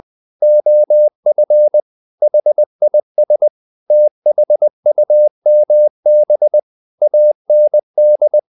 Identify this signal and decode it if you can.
Morse code, heard as a keyed tone, about 20 wpm, 600 Hz. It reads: OF HIS THUMB AND
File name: the old man and the sea / the old man and the sea part 1 20wpm